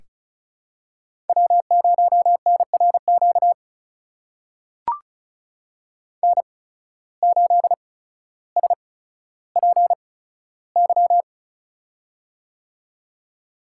Morse code, heard as a keyed tone, about 36 wpm, spaced as for 15 wpm, 700 Hz. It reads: W0NRQ N8SPY
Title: USA Call Signs - 5 Characters - 8x Character Spacing - 35wpm